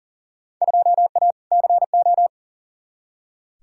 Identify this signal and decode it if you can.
Morse code, heard as a keyed tone, about 40 wpm, 700 Hz. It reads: 2A CO
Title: Field Day Exchanges - 40wpm